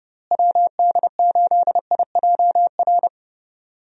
Morse code, heard as a keyed tone, about 30 wpm, 700 Hz. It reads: WD8IJR
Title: USA Call Signs - 6 Characters - Repeated 3x - 30wpm